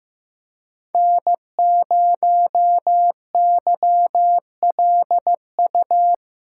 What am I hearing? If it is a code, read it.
Morse code, heard as a keyed tone, about 15 wpm, 700 Hz. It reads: N0YLU
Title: USA Call Signs - 8x Character Spacing - 15wpm